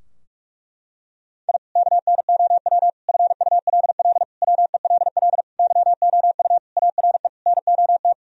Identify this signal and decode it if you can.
Morse code, heard as a keyed tone, about 46 wpm, 700 Hz. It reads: I KNOW FULL WELL YOU ARE NOT
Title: Sentences from Top 1000 Words - Review Part 3 - 45wpm